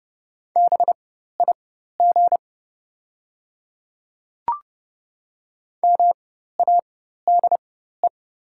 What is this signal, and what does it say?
Morse code, heard as a keyed tone, about 30 wpm, spaced as for 14 wpm, 700 Hz. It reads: BIG MADE
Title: 4x Character Spacing - Top 300 Words - 30wpm